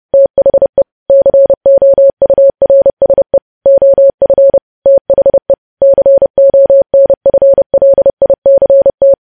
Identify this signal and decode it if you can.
Morse code, heard as a keyed tone, about 30 wpm, 550 Hz. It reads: THE COURSE OF THE CONFLICT
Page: news headlines in morse code at 30 wpm